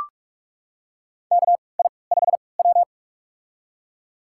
Morse code, heard as a keyed tone, about 41 wpm, spaced as for 25 wpm, 700 Hz. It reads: KI5W
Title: USA Call Signs - 4 Characters - 3x Character Spacing - 45wpm